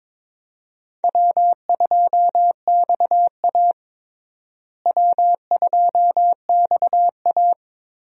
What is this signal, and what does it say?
Morse code, heard as a keyed tone, about 22 wpm, 700 Hz. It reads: W2XA W2XA
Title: USA Call Signs - 4 Characters - Repeated 2x - 22wpm